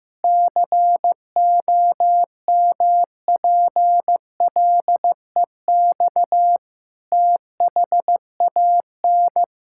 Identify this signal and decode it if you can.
Morse code, heard as a keyed tone, about 16 wpm, 700 Hz. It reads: COMPLEX THAN
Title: Sentences from Top 1000 Words - Review Part 6 - 15wpm